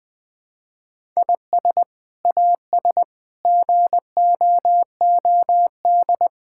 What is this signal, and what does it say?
Morse code, heard as a keyed tone, about 20 wpm, 700 Hz. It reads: IS AS GOOD
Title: Sets of 8 Words - 20wpm